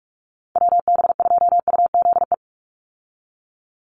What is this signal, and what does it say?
Morse code, heard as a keyed tone, about 44 wpm, 700 Hz. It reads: WB2UZE
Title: Call Signs - 45wpm